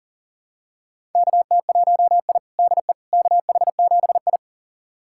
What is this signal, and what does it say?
Morse code, heard as a keyed tone, about 37 wpm, 700 Hz. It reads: KT1I DE KH7I
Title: USA Callsign DE Callsign - 4 Characters - 40wpm